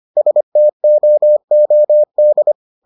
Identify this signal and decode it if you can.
Morse code, heard as a keyed tone, about 25 wpm, 600 Hz. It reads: STOOD